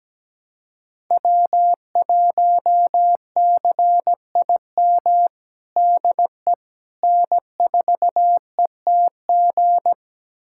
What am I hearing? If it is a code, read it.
Morse code, heard as a keyed tone, about 17 wpm, 700 Hz. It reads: W1CIM DE N4ETG
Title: USA Callsign DE Callsign - 5 Characters - 17wpm